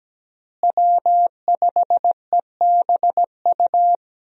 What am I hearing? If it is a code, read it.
Morse code, heard as a keyed tone, about 17 wpm, 700 Hz. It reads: W5EBU
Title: USA Call Signs - 5 Characters - Repeated 2x - 17wpm